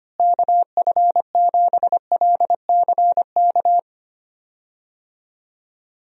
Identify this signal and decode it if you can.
Morse code, heard as a keyed tone, about 25 wpm, 700 Hz. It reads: KF7LCK
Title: USA Call Signs - 6 Characters - Repeated 3x - 25wpm